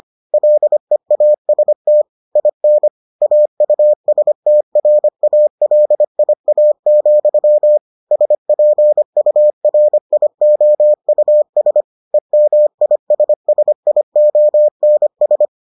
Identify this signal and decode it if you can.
Morse code, heard as a keyed tone, about 25 wpm, 600 Hz. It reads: LEAST IN AUSTRALIA, SPURIOUS EMISSIONS